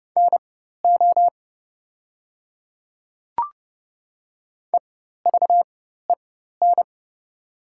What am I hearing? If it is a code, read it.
Morse code, heard as a keyed tone, about 28 wpm, spaced as for 14 wpm, 700 Hz. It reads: NO EVEN
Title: 4x Character Spacing - Top 200 Words - 30wpm